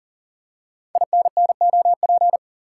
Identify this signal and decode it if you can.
Morse code, heard as a keyed tone, about 38 wpm, 700 Hz. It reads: INNOP
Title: Code Groups - 5 Letters - 40wpm